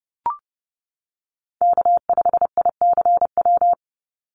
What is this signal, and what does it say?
Morse code, heard as a keyed tone, about 30 wpm, 700 Hz. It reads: K5ICW
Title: USA Call Signs 30wpm